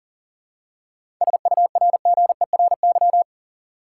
Morse code, heard as a keyed tone, about 37 wpm, 700 Hz. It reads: SURGERY
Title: Seven Letter Words - 40wpm